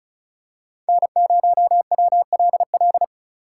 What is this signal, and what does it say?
Morse code, heard as a keyed tone, about 34 wpm, 700 Hz. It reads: N0WLL
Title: USA Call Signs - 2x Character Spacing - 35wpm